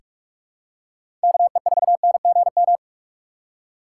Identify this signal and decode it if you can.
Morse code, heard as a keyed tone, about 42 wpm, 700 Hz. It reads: KE4NGM
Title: USA Call Signs - 6 Characters - 4x Character Spacing - 45wpm